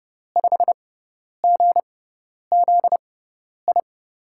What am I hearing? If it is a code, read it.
Morse code, heard as a keyed tone, about 30 wpm, spaced as for 10 wpm, 700 Hz. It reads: 5GZI
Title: USA Call Signs - 5 Characters - 6x Character Spacing - 30wpm